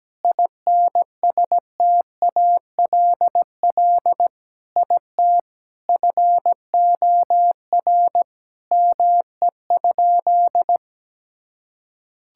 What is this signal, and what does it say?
Morse code, heard as a keyed tone, about 17 wpm, 700 Hz. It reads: INSTALL IT FOR ME?
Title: Sentences from Top 1600 Words 17wpm